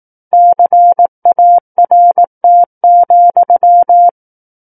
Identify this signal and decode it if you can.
Morse code, heard as a keyed tone, about 18 wpm, 700 Hz. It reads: CART,
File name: 18 WPM